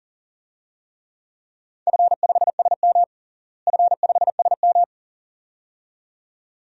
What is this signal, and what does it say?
Morse code, heard as a keyed tone, about 40 wpm, 700 Hz. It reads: F5SM F5SM
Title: DXCC-100 Call Signs - 4 Characters - Repeated 2x - 40wpm